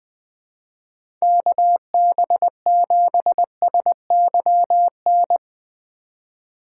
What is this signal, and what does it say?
Morse code, heard as a keyed tone, about 20 wpm, 700 Hz. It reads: KB7SYN
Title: USA Call Signs - 6 Characters - 2x Character Spacing - 20wpm